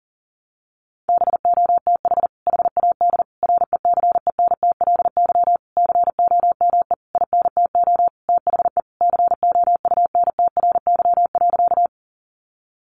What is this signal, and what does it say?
Morse code, heard as a keyed tone, about 40 wpm, 700 Hz. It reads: BOTH HAD RECENTLY COME INTO THE COUNTRY.
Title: Sentences from Top 1200 Words 40wpm